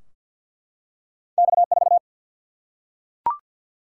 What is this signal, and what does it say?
Morse code, heard as a keyed tone, about 51 wpm, 700 Hz. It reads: X4
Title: Single-Letter Single-Number 50wpm